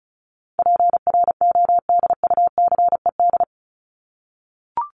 Morse code, heard as a keyed tone, about 34 wpm, 700 Hz. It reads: PRODUCED
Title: ED Suffix - Top 100 Words with Base Words 35wpm